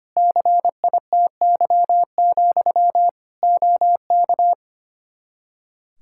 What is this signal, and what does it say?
Morse code, heard as a keyed tone, about 25 wpm, 700 Hz. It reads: CITY, OK
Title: Practice QSO - Exchange 1 - Beginning - 25wpm